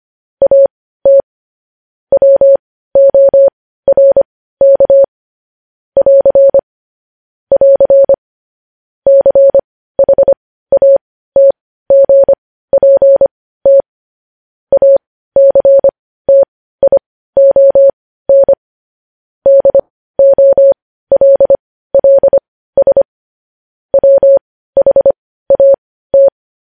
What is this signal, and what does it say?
Morse code, heard as a keyed tone, about 25 wpm, spaced as for 15 wpm, 550 Hz. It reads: AT WORK <AR> <AR> CHATGPT ACTION DOLLS WHAT
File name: podcast / News headlines-1x-25x15